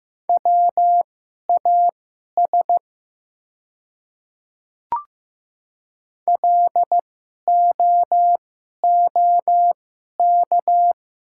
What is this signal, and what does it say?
Morse code, heard as a keyed tone, about 15 wpm, spaced as for 11 wpm, 700 Hz. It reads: WAS LOOK
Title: 2x Character Spacing - Top 100 Words - 15wpm